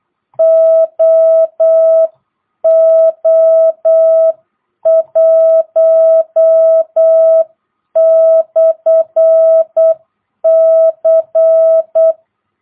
Morse code, heard as a keyed tone, about 8 wpm, 660 Hz. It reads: OO1/C
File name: jh8cbh / 2022cw